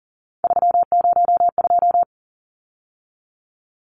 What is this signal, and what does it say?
Morse code, heard as a keyed tone, about 40 wpm, 700 Hz. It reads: 302
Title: Numbers - 3-Digit 40wpm